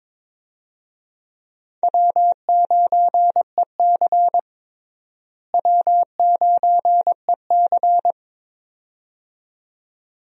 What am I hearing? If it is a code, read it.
Morse code, heard as a keyed tone, about 22 wpm, 700 Hz. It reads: W9EC W9EC
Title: USA Call Signs - 4 Characters - Repeated 2x - 22wpm